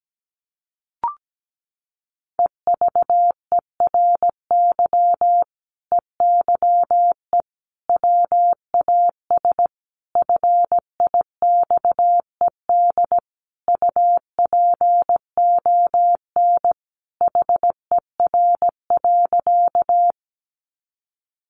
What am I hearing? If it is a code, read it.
Morse code, heard as a keyed tone, about 17 wpm, 700 Hz. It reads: EVERY EYE WAS FIXED UPON HER.
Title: Sentences from Top 1100 Words 17wpm